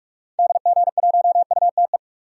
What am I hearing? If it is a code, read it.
Morse code, heard as a keyed tone, about 47 wpm, 700 Hz. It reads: DG1UTE
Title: DXCC-100 Call Signs - 6 Characters - 45wpm